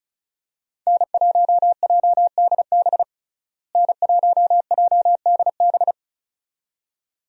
Morse code, heard as a keyed tone, about 34 wpm, 700 Hz. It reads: N1JDB N1JDB
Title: USA Call Signs - 5 Characters - Repeated 2x - 35wpm